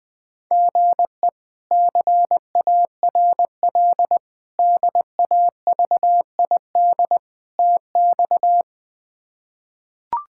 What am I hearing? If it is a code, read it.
Morse code, heard as a keyed tone, about 21 wpm, 700 Hz. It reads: GE CARL DAVID TX
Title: K1USN SST QSO Practice - 20wpm